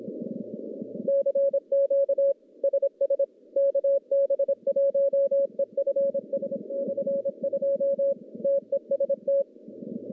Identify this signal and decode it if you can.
Morse code, heard as a keyed tone, about 26 wpm, 550 Hz. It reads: CQ SS KB1EFS/2 TEST
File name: contests / CW-SS-k5tr- -1100